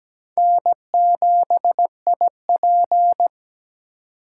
Morse code, heard as a keyed tone, about 20 wpm, 700 Hz. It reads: N7IP
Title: Speed Racing - USA 1x2 Call Signs 17wpm